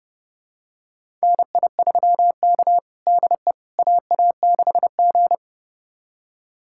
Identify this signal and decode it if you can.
Morse code, heard as a keyed tone, about 30 wpm, 700 Hz. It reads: NI3K DE AA6G
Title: USA Callsign DE Callsign - 4 Characters - 30wpm